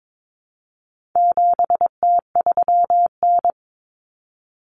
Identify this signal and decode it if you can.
Morse code, heard as a keyed tone, about 22 wpm, 700 Hz. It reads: 7T3N
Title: Common Cut Numbers - 4-Digit 22wpm